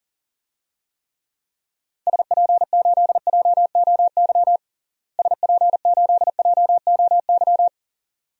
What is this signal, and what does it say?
Morse code, heard as a keyed tone, about 37 wpm, 700 Hz. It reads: SP8JOY SP8JOY
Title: DXCC-100 Call Signs - 6 Characters - Repeated 2x - 40wpm